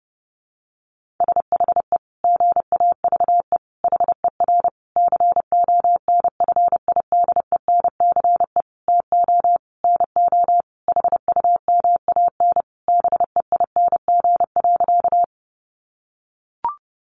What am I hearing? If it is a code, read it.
Morse code, heard as a keyed tone, about 30 wpm, 700 Hz. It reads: SHE GAVE HER CONFIDENCE TO NO HUMAN BEING.
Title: Sentences from Top 1500 Words 30wpm